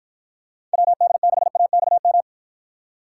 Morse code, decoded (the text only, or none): WD6AXM